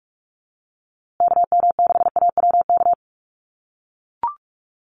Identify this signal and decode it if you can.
Morse code, heard as a keyed tone, about 44 wpm, 700 Hz. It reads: KM6AWK
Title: USA Call Signs Repeated 2x 45wpm